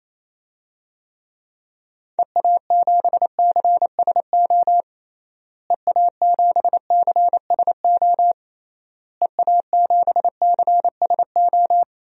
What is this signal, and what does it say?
Morse code, heard as a keyed tone, about 28 wpm, 700 Hz. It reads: EA7CSO EA7CSO EA7CSO
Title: DXCC-100 Call Signs - 6 Characters - Repeated 3x - 28wpm